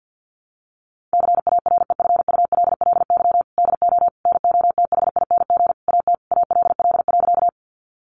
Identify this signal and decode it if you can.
Morse code, heard as a keyed tone, about 50 wpm, 700 Hz. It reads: CAREFULLY DO NOTHING AT ALL.
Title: Sentences from Top 1300 Words 50wpm